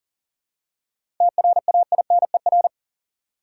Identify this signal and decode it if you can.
Morse code, heard as a keyed tone, about 40 wpm, 700 Hz. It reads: TRAINER
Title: Seven Letter Words - 40wpm